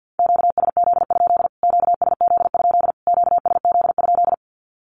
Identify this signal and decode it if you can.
Morse code, heard as a keyed tone, about 51 wpm, 700 Hz. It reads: QSZ? QSZ? QSZ?
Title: Q-Codes Repeated 3x 50wpm